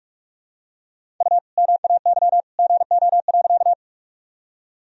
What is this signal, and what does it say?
Morse code, heard as a keyed tone, about 46 wpm, 700 Hz. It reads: U MAY GO.
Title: Sentences from Top 100 Words and Abbreviations 45wpm